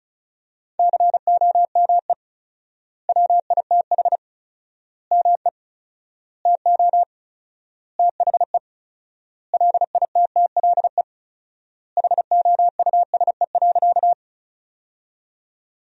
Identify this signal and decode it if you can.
Morse code, heard as a keyed tone, about 36 wpm, 700 Hz. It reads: COME WITH ME TO THE LITTLE HOUSE.
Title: 4x Word-Spacing - Sentences from Top 200 Words - 35wpm